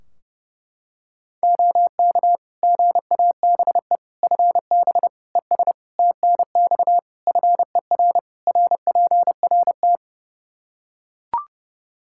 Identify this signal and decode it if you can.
Morse code, heard as a keyed tone, about 30 wpm, 700 Hz. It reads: OK GABE FB ES TNX FER RPRT
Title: Practice QSO - Exchange 2 - Info - 30wpm